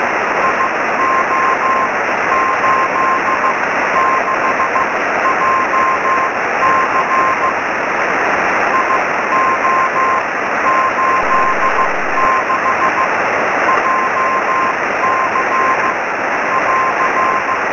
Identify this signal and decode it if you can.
Morse code, heard as a keyed tone, about 15 wpm, spaced as for 12 wpm, 1060 Hz, with noise. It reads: IO86JC IO86JC <AR>
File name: sm7lcb / gm3sbc 23cm